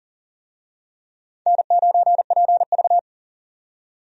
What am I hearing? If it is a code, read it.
Morse code, heard as a keyed tone, about 40 wpm, 700 Hz. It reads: N9PV
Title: USA Call Signs - 4 Characters - 40wpm